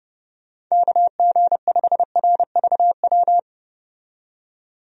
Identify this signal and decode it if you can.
Morse code, heard as a keyed tone, about 32 wpm, 700 Hz. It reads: KG5RVW
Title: USA Call Signs - 6 Characters - 30wpm